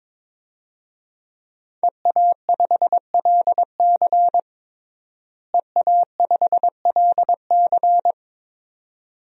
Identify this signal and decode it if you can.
Morse code, heard as a keyed tone, about 22 wpm, 700 Hz. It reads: EA5LC EA5LC
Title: DXCC-100 Call Signs - 5 Characters - Repeated 2x - 22wpm